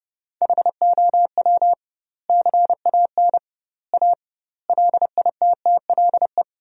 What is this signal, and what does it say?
Morse code, heard as a keyed tone, about 30 wpm, 700 Hz. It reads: HOW CAN A LITTLE
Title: 2x Word-Spacing - Sentences from Top 500 Words - 30wpm